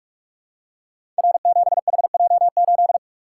Morse code, heard as a keyed tone, about 44 wpm, 700 Hz. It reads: R7HJ8
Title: Code Groups - 5 Letter:Numbers - 45wpm